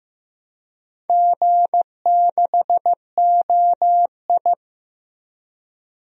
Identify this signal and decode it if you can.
Morse code, heard as a keyed tone, about 15 wpm, 700 Hz. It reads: G6OI
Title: DXCC-100 Call Signs - 4 Characters - 15wpm